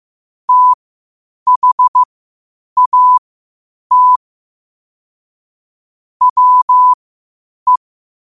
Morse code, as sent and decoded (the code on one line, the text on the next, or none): - .... .- - / .-- .
THAT WE